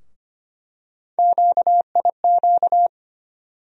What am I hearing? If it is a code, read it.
Morse code, heard as a keyed tone, about 25 wpm, 700 Hz. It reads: QIQ
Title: USA Call Sign Postfixes - 3 Letters 25wpm